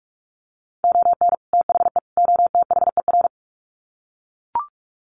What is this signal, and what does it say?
Morse code, heard as a keyed tone, about 44 wpm, 700 Hz. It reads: ON THE OTHER
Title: Sets of 3 Words 45wpm